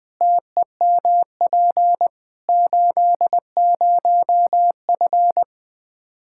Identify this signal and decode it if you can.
Morse code, heard as a keyed tone, about 21 wpm, 700 Hz. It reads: TEMP 80F
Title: Practice QSO - Exchange 2 - Info - 20wpm